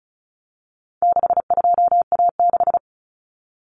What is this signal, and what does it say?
Morse code, heard as a keyed tone, about 34 wpm, 700 Hz. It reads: 62A6
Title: Common Cut Numbers - 4-Digit 35wpm